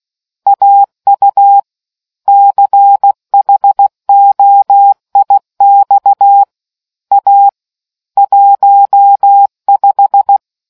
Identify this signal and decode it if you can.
Morse code, heard as a keyed tone, about 16 wpm, 800 Hz. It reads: AU CHOIX A 15